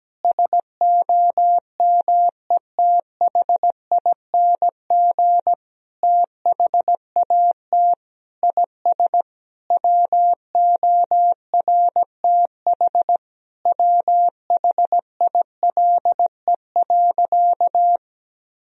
Morse code, heard as a keyed tone, about 17 wpm, 700 Hz. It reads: SOMETHING THAT IS WORTH WHILE.
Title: Sentences from Top 1000 Words - Review Part 1 - 17wpm